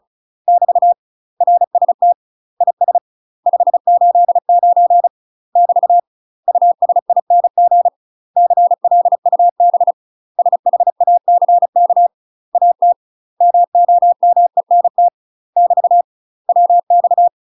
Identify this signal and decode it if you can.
Morse code, heard as a keyed tone, about 34 wpm, 700 Hz. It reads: <BT> RST IS 589 <BT> USING CLUB SHACK AT MOMENT <BT> WX
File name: UK MASTER QSOs TRAINING / 35WPM / UK MASTER QSOs 0043